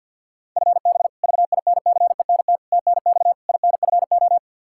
Morse code, heard as a keyed tone, about 49 wpm, 700 Hz. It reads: FB VINCENT TNX INFO
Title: Practice QSO - Exchange 3 - More Info - 50wpm